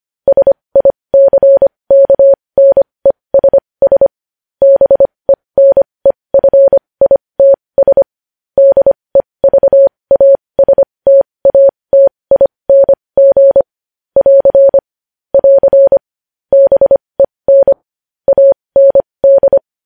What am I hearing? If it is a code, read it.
Morse code, heard as a keyed tone, about 25 wpm, spaced as for 20 wpm, 550 Hz. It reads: SICKNESS BENEFITS DEVASTATING <AR> <AR> BEN AND
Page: news-headlines-in-morse-code-at-20-wpm